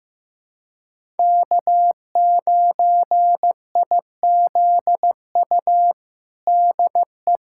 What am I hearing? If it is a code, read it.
Morse code, heard as a keyed tone, about 15 wpm, 700 Hz. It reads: K9IZU DE
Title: USA Callsign DE Callsign - 5 Characters - 15wpm